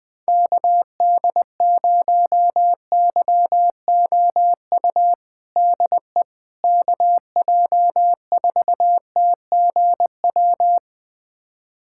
Morse code, heard as a keyed tone, about 21 wpm, 700 Hz. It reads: KD0YOU DE KJ4TGW
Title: USA Callsign DE Callsign - 6 Characters - 20wpm